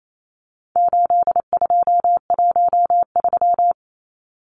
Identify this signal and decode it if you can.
Morse code, heard as a keyed tone, about 28 wpm, 700 Hz. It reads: 8213
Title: Numbers - 4-Digit 28wpm